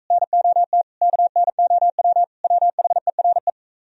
Morse code, heard as a keyed tone, about 39 wpm, 700 Hz. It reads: NOT KNOW WHERE